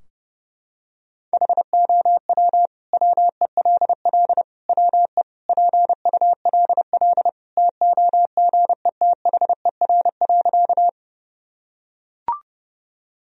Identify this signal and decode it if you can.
Morse code, heard as a keyed tone, about 28 wpm, 700 Hz. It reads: HOW WELL WE PULL TOGETHER.
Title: Sentences from Top 1000 Words - Review Part 2 - 30wpm